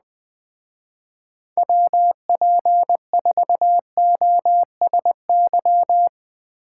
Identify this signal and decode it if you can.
Morse code, heard as a keyed tone, about 19 wpm, 700 Hz. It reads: WP4OSY
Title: Speed Racing - USA 2x3 Call Signs 17wpm